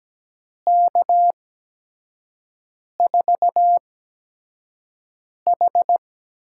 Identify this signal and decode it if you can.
Morse code, heard as a keyed tone, about 17 wpm, spaced as for 5 wpm, 700 Hz. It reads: K4H
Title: USA Call Signs - 6 Characters - 8x Character Spacing - 17wpm